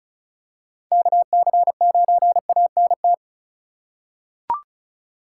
Morse code, heard as a keyed tone, about 36 wpm, 700 Hz. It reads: KC9ANT
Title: USA Call Signs - 4x Character Spacing - 35wpm